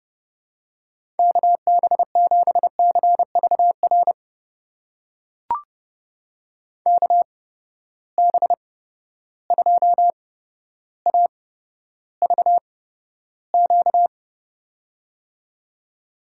Morse code, heard as a keyed tone, about 30 wpm, spaced as for 13 wpm, 700 Hz. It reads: KB7CVR KB2AVQ
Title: USA Call Signs - 6 Characters - 8x Character Spacing - 30wpm